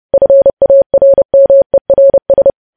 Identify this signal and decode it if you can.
Morse code, heard as a keyed tone, about 30 wpm, 550 Hz. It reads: FARMERS
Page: news headlines in morse code at 30 wpm